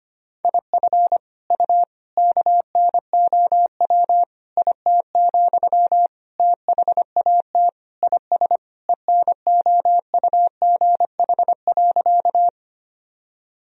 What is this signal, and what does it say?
Morse code, heard as a keyed tone, about 25 wpm, 700 Hz. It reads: IF U KNOW IT, THAT IS ENOUGH.
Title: Sentences from Top 400 Words and Abbreviations 25wpm